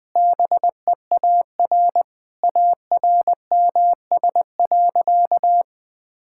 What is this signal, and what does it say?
Morse code, heard as a keyed tone, about 20 wpm, 700 Hz. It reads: BEAR ARMS.
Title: Sentences with 8-Word Phrases - 20wpm